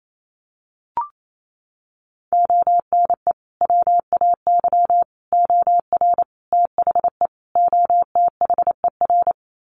Morse code, heard as a keyed tone, about 28 wpm, 700 Hz. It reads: ONE WAY OR THE OTHER
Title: Sets of 5 Words 28wpm